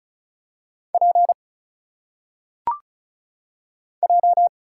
Morse code, heard as a keyed tone, about 35 wpm, 700 Hz. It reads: P J
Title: Single Letters - Phonetic Pronunciation 35wpm